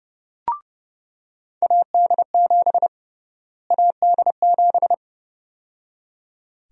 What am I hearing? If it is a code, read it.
Morse code, heard as a keyed tone, about 30 wpm, 700 Hz. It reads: AD7 AD7
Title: USA Call Sign Prefixes - Repeated 2x - 30wpm